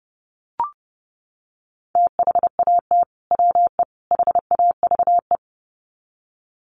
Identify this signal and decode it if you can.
Morse code, heard as a keyed tone, about 30 wpm, 700 Hz. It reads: THAT WE HAVE
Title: Sets of 3 Words 30wpm